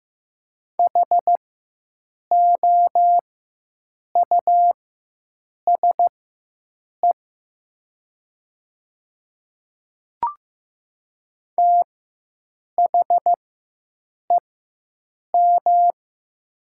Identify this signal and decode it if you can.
Morse code, heard as a keyed tone, about 15 wpm, spaced as for 7 wpm, 700 Hz. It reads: HOUSE THEM
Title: 4x Character Spacing - Top 200 Words - 15wpm